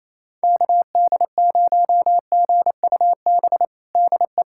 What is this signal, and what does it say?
Morse code, heard as a keyed tone, about 28 wpm, 700 Hz. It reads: KD0GUB DE
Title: USA Callsign DE Callsign 28wpm